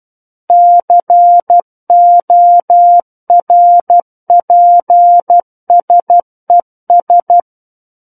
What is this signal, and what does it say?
Morse code, heard as a keyed tone, about 12 wpm, 700 Hz. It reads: CORPSES
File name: Book One / 03-Chapter 3 - 12 WPM